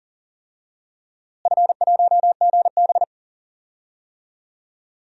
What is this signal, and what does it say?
Morse code, heard as a keyed tone, about 40 wpm, 700 Hz. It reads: F1GB
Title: DXCC-100 Call Signs - 4 Characters - Repeated 2x - 40wpm